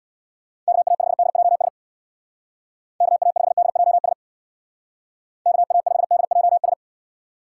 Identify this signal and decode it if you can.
Morse code, heard as a keyed tone, about 62 wpm, 700 Hz. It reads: KA5DJS KA5DJS KA5DJS
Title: Speed Racing - USA 2x3 Call Signs 50wpm